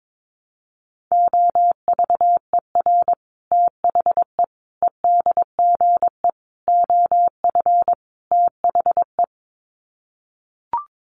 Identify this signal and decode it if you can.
Morse code, heard as a keyed tone, about 22 wpm, 700 Hz. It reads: OVER THE EDGE OF THE
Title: Sets of 5 Words Encore2 22wpm